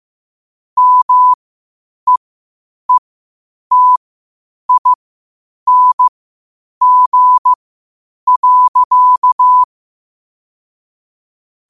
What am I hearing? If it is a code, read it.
Morse code, heard as a keyed tone, about 15 wpm, spaced as for 8 wpm, 1000 Hz. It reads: MEETING.
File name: qotdfiles / 8wpm